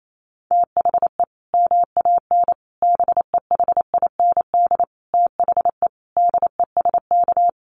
Morse code, heard as a keyed tone, about 28 wpm, 700 Hz. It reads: THE MAN BEHIND THE DESK
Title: A Sound Of Thunder 28wpm